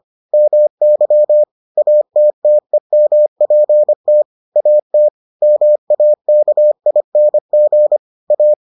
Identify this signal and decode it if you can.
Morse code, heard as a keyed tone, about 25 wpm, 600 Hz. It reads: MY ATTEMPT AT MAKING A